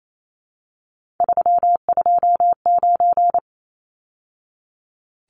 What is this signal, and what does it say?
Morse code, heard as a keyed tone, about 28 wpm, 700 Hz. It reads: 329
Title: Numbers - 3-Digit 28wpm